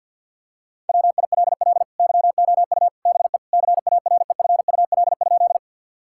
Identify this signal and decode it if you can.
Morse code, heard as a keyed tone, about 50 wpm, 700 Hz. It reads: WILL YOU BE CAREFUL?
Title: Sentences from Top 1600 Words 50wpm